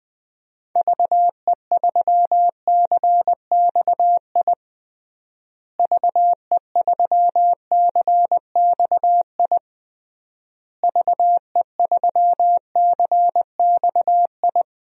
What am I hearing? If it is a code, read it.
Morse code, heard as a keyed tone, about 20 wpm, 700 Hz. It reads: VE3CXI VE3CXI VE3CXI
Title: DXCC-100 Call Signs - 6 Characters - Repeated 3x - 20wpm